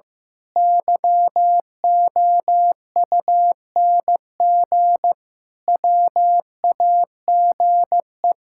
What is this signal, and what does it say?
Morse code, heard as a keyed tone, about 15 wpm, 700 Hz. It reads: YOUNG WAGE
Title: Pangram Challenge v2 - 15wpm